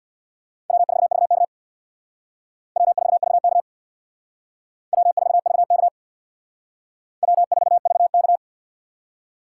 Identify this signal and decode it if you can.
Morse code, heard as a keyed tone, about 61 wpm, 700 Hz. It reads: W4VK W4VK W4VK W4VK
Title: Speed Racing - USA 1x2 Call Signs 50wpm